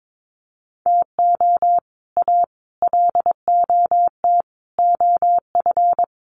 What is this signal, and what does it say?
Morse code, heard as a keyed tone, about 22 wpm, 700 Hz. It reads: TO A LOT OF
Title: Sets of 5 Words Encore 22wpm